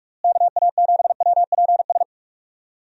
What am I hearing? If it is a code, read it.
Morse code, heard as a keyed tone, about 44 wpm, 700 Hz. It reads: KA7WPS
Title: USA Call Signs - 6 Characters - Repeated 2x - 45wpm